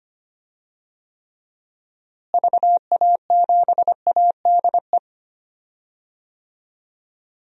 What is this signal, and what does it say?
Morse code, heard as a keyed tone, about 25 wpm, 700 Hz. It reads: VA7ADE
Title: DXCC-100 Call Signs - 6 Characters - Repeated 2x - 25wpm